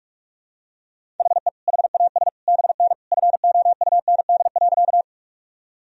Morse code, heard as a keyed tone, about 46 wpm, 700 Hz. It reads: HE HAS BN FOUND.
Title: Sentences from Top 300 Words and Abbreviations 45wpm